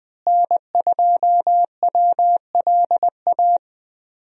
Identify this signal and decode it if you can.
Morse code, heard as a keyed tone, about 20 wpm, 700 Hz. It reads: N2WLA
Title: USA Call Signs - 5 Characters - Repeated 2x - 20wpm